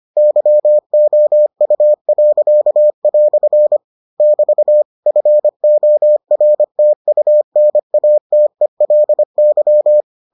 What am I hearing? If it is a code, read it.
Morse code, heard as a keyed tone, about 25 wpm, 600 Hz. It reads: YOU." <BT> FORTUNATELY